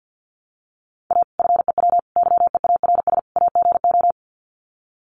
Morse code, heard as a keyed tone, about 50 wpm, 700 Hz. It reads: A FEW YEARS AGO
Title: Sets of 4 Words 50wpm